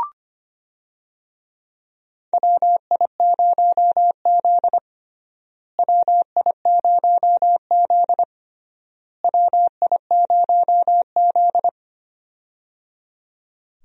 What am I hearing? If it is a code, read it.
Morse code, heard as a keyed tone, about 25 wpm, 700 Hz. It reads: WI0Z WI0Z WI0Z
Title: USA Call Signs - 4 Characters - Repeated 3x - 25wpm